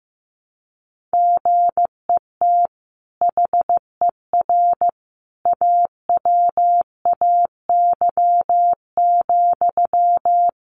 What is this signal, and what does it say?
Morse code, heard as a keyed tone, about 15 wpm, 700 Hz. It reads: GET HER AWAY,
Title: Sentences from Top 1500 Words 15wpm